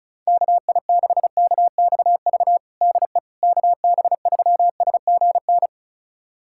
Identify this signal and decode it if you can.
Morse code, heard as a keyed tone, about 36 wpm, 700 Hz. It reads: KI6KXV DE KB3SGN
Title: USA Callsign DE Callsign 35wpm